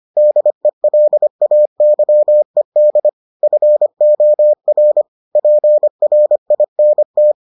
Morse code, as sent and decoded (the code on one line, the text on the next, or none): -.. .-.. .- -.-- . -.. / ..-. --- .-. / .--. .-. .. -. -
DELAYED FOR PRINT